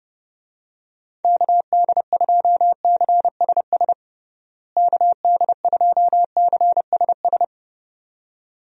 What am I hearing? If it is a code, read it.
Morse code, heard as a keyed tone, about 29 wpm, 700 Hz. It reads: KD2CSS KD2CSS
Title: USA Call Signs - 6 Characters - Repeated 2x - 30wpm